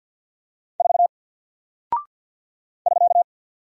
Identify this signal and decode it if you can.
Morse code, heard as a keyed tone, about 50 wpm, 700 Hz. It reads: 4 <SK>
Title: Single Character (Letter:Number:Punctuation:Prosign) Alt 50wpm